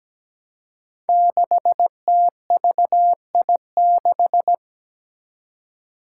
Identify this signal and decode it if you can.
Morse code, heard as a keyed tone, about 17 wpm, 700 Hz. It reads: 6TVI6
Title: Five Challenging Letter-Number Combinations 17wpm